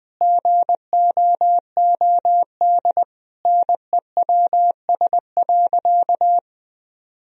Morse code, heard as a keyed tone, about 21 wpm, 700 Hz. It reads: GOOD NEWS.
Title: Sentences from Top 1000 Words - Review Part 3 - 20wpm